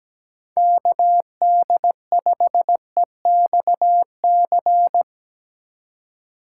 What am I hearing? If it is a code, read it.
Morse code, heard as a keyed tone, about 17 wpm, 700 Hz. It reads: KD5EXC
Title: USA Call Signs - 6 Characters - 4x Character Spacing - 17wpm